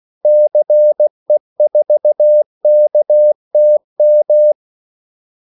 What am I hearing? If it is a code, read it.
Morse code, heard as a keyed tone, about 16 wpm, 600 Hz. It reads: CE4KTM
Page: Kaldesignaler 16 wpm